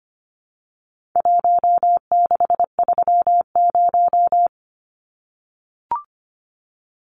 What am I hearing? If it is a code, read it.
Morse code, heard as a keyed tone, about 25 wpm, 700 Hz. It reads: 1630
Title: Numbers - Contest Style - 4-Digit 25wpm